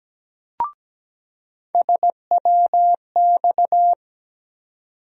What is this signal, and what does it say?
Morse code, heard as a keyed tone, about 17 wpm, 700 Hz. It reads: SWX
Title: USA Call Sign Postfixes - 3 Letters 17wpm